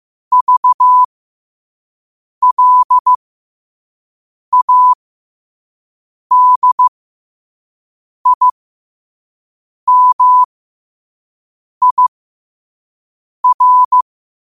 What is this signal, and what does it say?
Morse code, heard as a keyed tone, about 15 wpm, spaced as for 5 wpm, 1000 Hz. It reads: VLADIMIR
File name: qotdfiles / 5wpm